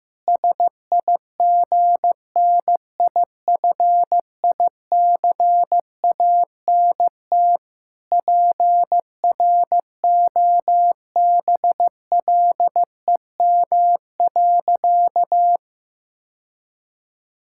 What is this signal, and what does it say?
Morse code, heard as a keyed tone, about 15 wpm, 700 Hz. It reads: SIGNIFICANT PROBLEM.
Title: Sentences from Top 1000 Words - Review Part 6 - 15wpm